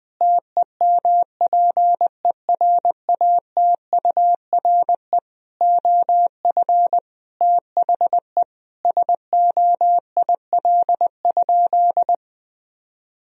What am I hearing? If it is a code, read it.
Morse code, heard as a keyed tone, about 20 wpm, 700 Hz. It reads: TEMPERATURE OF THE SOIL?
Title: Sentences from Top 1000 Words - Review Part 4 - 20wpm